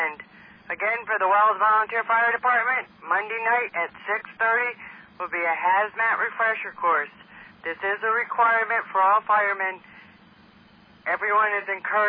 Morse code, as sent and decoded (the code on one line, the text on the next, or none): none